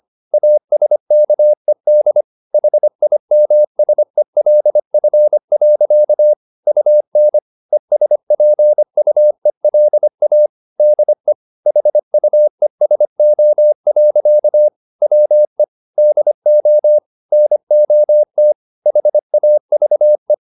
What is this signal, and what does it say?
Morse code, heard as a keyed tone, about 25 wpm, 600 Hz. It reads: ASKED HIMSELF. UN ESPUELA DE HUESO. WE DO NOT HAVE
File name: the old man and the sea part 1 25wpm